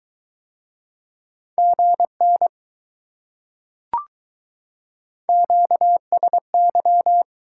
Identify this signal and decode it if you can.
Morse code, heard as a keyed tone, about 22 wpm, 700 Hz. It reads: GN QSY